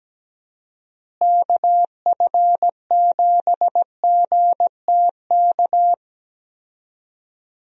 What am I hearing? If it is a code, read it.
Morse code, heard as a keyed tone, about 17 wpm, 700 Hz. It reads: KF7GTK
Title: USA Call Signs - 6 Characters - Repeated 2x - 17wpm